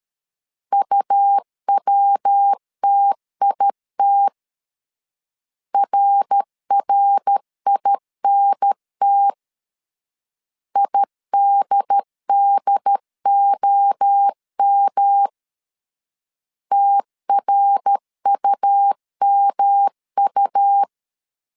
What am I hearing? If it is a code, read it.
Morse code, heard as a keyed tone, about 13 wpm, 800 Hz. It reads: UWTIT RRINT IDDOM TRUMU
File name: CW / DET / 1 3 1500C S C 12wpm 2750